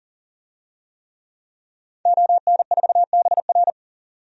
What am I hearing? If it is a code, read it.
Morse code, heard as a keyed tone, about 38 wpm, 700 Hz. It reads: ON4BR
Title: DXCC-100 Call Signs - 5 Characters - 40wpm